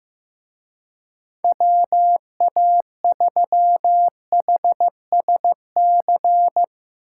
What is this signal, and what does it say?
Morse code, heard as a keyed tone, about 15 wpm, 700 Hz. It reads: WA3HSC
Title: Speed Racing - USA 2x3 Call Signs 15wpm